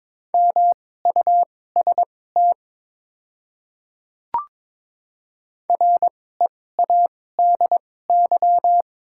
Morse code, heard as a keyed tone, about 22 wpm, spaced as for 16 wpm, 690 Hz. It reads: MUST READY
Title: 2x Character Spacing - Top 500 Words - 22wpm